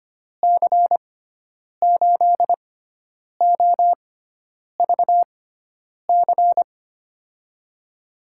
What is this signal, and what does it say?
Morse code, heard as a keyed tone, about 25 wpm, spaced as for 9 wpm, 700 Hz. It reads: C8OVC
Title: USA Call Signs - 6x Character Spacing - 25wpm